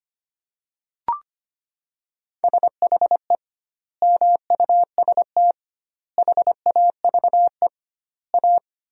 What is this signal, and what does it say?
Morse code, heard as a keyed tone, about 25 wpm, 700 Hz. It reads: SHE MUST HAVE A
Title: 2x Word-Spacing - Sentences from Top 300 Words - 25wpm